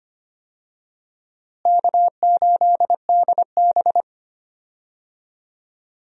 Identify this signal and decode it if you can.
Morse code, heard as a keyed tone, about 25 wpm, 700 Hz. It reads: K8DB
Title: USA Call Signs - 4 Characters - Repeated 2x - 25wpm